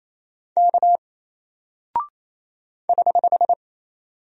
Morse code, heard as a keyed tone, about 27 wpm, 700 Hz. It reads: K <HH>
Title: Single Character (Letter:Number:Punctuation:Prosign) 28wpm